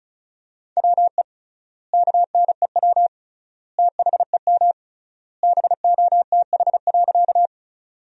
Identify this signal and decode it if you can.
Morse code, heard as a keyed tone, about 36 wpm, 700 Hz. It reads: WE KNEW THEM BOTH.
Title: 3x Word-Spacing - Sentences from Top 400 Words - 35wpm